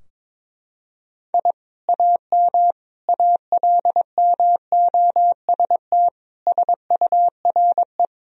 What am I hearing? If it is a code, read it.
Morse code, heard as a keyed tone, about 22 wpm, 700 Hz. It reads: I AM ALMOST SURE
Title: Sentences from Top 1000 Words - Review Part 4 - 22wpm